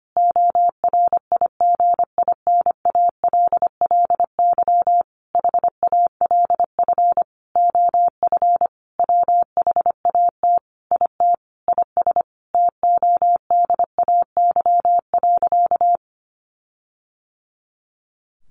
Morse code, heard as a keyed tone, about 25 wpm, 700 Hz. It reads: ORIGINALLY HALF OF WHAT IT IS TODAY.
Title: Sentences from Top 1500 Words 25wpm